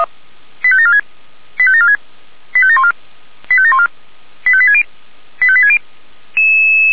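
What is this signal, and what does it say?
Five-tone selective call (ZVEI), heard as quick runs of five tones, tones of about 80 ms. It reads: 85746 85746 85714 85714 85860 85860
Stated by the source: ZVEI five-tone selective call received by SDR.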